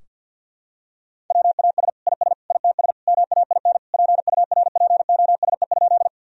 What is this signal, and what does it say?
Morse code, heard as a keyed tone, about 53 wpm, 700 Hz. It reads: WAS IS ITS MAIN PURPOSE?
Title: Sentences from Top 1000 Words - Review Part 2 - 50wpm